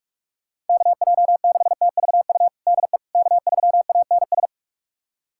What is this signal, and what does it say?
Morse code, heard as a keyed tone, about 43 wpm, 700 Hz. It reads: KJ6TVU DE K3ANS
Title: USA Callsign DE Callsign 45wpm